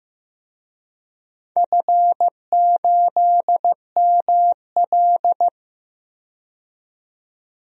Morse code, heard as a keyed tone, about 15 wpm, 700 Hz. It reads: F8ML
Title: DXCC-100 Call Signs - 4 Characters - 15wpm